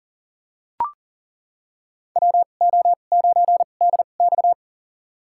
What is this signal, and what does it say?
Morse code, heard as a keyed tone, about 38 wpm, spaced as for 28 wpm, 700 Hz. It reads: WO9DX
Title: USA Call Signs - 5 Characters - 2x Character Spacing - 40wpm